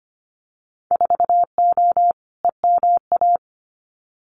Morse code, heard as a keyed tone, about 25 wpm, 700 Hz. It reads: 4O EMA
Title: Winter Field Day Exchanges Encore 25wpm